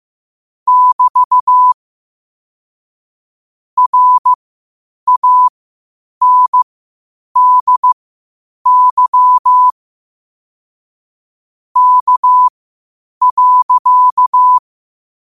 Morse code, heard as a keyed tone, about 15 wpm, spaced as for 8 wpm, 1000 Hz. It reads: <BT> RANDY K.
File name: qotdfiles / 8wpm